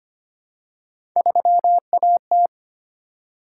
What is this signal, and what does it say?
Morse code, heard as a keyed tone, about 25 wpm, 700 Hz. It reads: 3AT
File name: Three Letter-Number Combinations - 25wpm